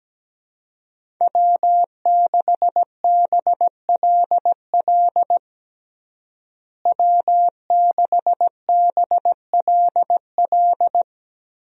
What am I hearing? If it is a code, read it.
Morse code, heard as a keyed tone, about 17 wpm, 700 Hz. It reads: W6BLL W6BLL
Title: USA Call Signs - 5 Characters - Repeated 2x - 17wpm